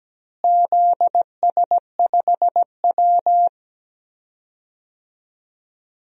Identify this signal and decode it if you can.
Morse code, heard as a keyed tone, about 17 wpm, 700 Hz. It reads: ZS5W
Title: DXCC-100 Call Signs - 4 Characters - Repeated 2x - 17wpm